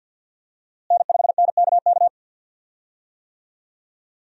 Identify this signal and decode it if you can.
Morse code, heard as a keyed tone, about 52 wpm, 700 Hz. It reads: N5NKK
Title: USA Call Signs - 5 Characters - Repeated 2x - 50wpm